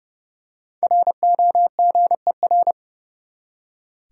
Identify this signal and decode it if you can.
Morse code, heard as a keyed tone, about 30 wpm, 700 Hz. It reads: ROGER
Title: USA Common Names - Max 5 Letters - 30wpm